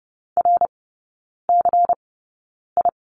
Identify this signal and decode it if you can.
Morse code, heard as a keyed tone, about 30 wpm, spaced as for 9 wpm, 700 Hz. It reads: RCI
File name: Lesson 16e - Letter c - Three Characters 30wpm